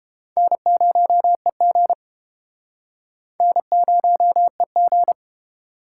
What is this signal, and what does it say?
Morse code, heard as a keyed tone, about 33 wpm, 700 Hz. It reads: N0EG N0EG